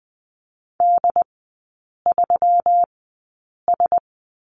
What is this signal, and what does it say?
Morse code, heard as a keyed tone, about 20 wpm, spaced as for 8 wpm, 700 Hz. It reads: D3S
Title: 127 Lesson 21e - Number 6 - Three Characters 20wpm